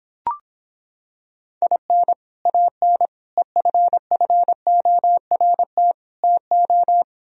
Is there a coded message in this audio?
Yes: Morse code from 1.6 to 7.0 s, heard as a keyed tone, about 26 wpm, 700 Hz. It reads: IN AN EFFORT TO